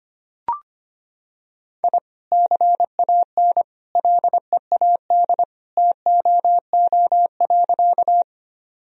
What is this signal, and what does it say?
Morse code, heard as a keyed tone, about 25 wpm, 700 Hz. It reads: I CAN LEAD TOO.
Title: Sentences from Top 1000 Words - Review Part 2 - 25wpm